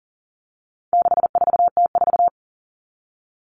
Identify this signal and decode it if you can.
Morse code, heard as a keyed tone, about 40 wpm, 700 Hz. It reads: 64T4
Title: All Cut Numbers - 4-Digit 40wpm